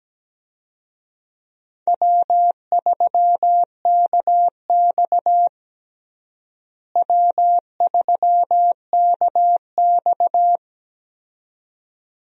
Morse code, heard as a keyed tone, about 17 wpm, 700 Hz. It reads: W3KX W3KX
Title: USA Call Signs - 4 Characters - Repeated 2x - 17wpm